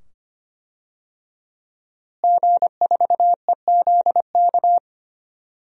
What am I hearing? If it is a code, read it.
Morse code, heard as a keyed tone, about 25 wpm, 700 Hz. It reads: G4EZK
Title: DXCC-100 Call Signs - 5 Characters - Repeated 2x - 25wpm